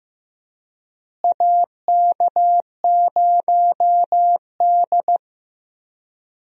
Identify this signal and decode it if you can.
Morse code, heard as a keyed tone, about 15 wpm, 700 Hz. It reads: AK0D
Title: USA Call Signs - 4 Characters - Repeated 3x - 15wpm